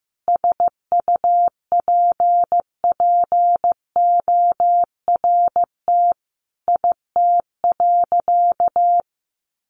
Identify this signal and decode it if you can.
Morse code, heard as a keyed tone, about 15 wpm, 700 Hz. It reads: SUPPORT IT.
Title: Sentences from Top 1300 Words 15wpm